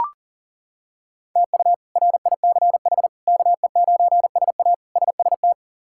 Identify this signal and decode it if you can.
Morse code, heard as a keyed tone, about 40 wpm, 700 Hz. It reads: TU RICH KE9SA SST
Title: K1USN SST QSO Practice - 40wpm